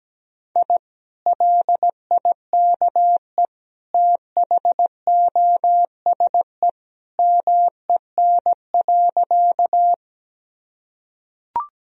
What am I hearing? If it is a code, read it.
Morse code, heard as a keyed tone, about 17 wpm, 700 Hz. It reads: I LIKE THOSE MEN.
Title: Sentences from Top 1000 Words - Review Part 1 - 17wpm